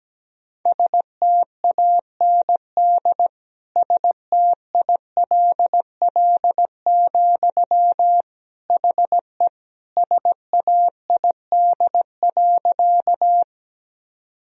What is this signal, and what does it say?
Morse code, heard as a keyed tone, about 17 wpm, 700 Hz. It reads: STAND STILL, HE SAID.
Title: Sentences from Top 1000 Words - Review Part 1 - 17wpm